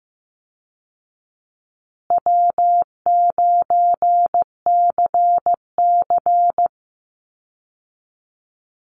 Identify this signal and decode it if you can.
Morse code, heard as a keyed tone, about 15 wpm, 700 Hz. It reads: W9CC
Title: Easy Call Signs Repeated 3x - 15wpm